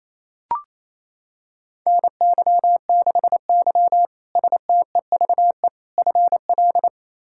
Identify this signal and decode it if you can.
Morse code, heard as a keyed tone, about 28 wpm, 700 Hz. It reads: NY6Y STEVE FL
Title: K1USN SST QSO Practice - 28wpm